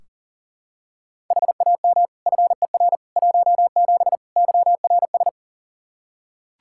Morse code, heard as a keyed tone, about 37 wpm, 690 Hz. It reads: HAM FER 17 YRS
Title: Practice QSO - Exchange 3 - More Info - 40wpm